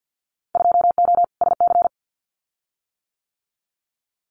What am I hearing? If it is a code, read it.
Morse code, heard as a keyed tone, about 50 wpm, 700 Hz. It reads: 2O SC
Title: Winter Field Day Exchanges 50wpm